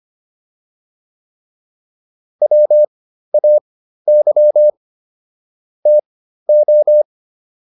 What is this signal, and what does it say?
Morse code, heard as a keyed tone, about 25 wpm, spaced as for 13 wpm, 600 Hz. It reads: WAY TO